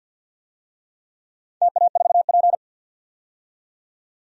Morse code, heard as a keyed tone, about 52 wpm, 700 Hz. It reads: TA4P
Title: DXCC-100 Call Signs - 4 Characters - 50wpm